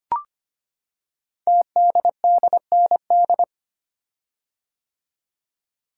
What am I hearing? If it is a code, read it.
Morse code, heard as a keyed tone, about 25 wpm, 700 Hz. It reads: TDDND
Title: Code Groups - 5 Letters - 25wpm